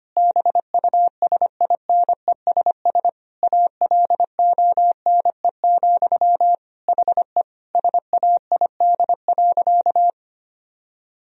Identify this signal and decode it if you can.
Morse code, heard as a keyed tone, about 25 wpm, 700 Hz. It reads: BUSINESS ALONE, HE SAID.
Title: Sentences from Top 1000 Words - Review Part 2 - 25wpm